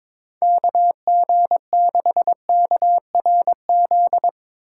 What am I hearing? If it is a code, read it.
Morse code, heard as a keyed tone, about 21 wpm, 700 Hz. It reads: KG6KRZ
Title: USA Call Signs - 6 Characters - 2x Character Spacing - 22wpm